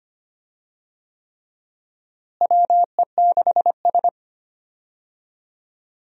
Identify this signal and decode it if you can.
Morse code, heard as a keyed tone, about 25 wpm, 700 Hz. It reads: WE6S
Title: USA Call Signs - 4 Characters - Repeated 3x - 25wpm